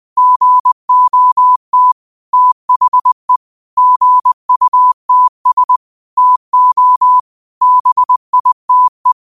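Morse code, heard as a keyed tone, about 20 wpm, 1000 Hz. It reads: GOT THE GUTS TO BITE
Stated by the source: 20 WPM Morse code quotes, QOTD at 20 WPM.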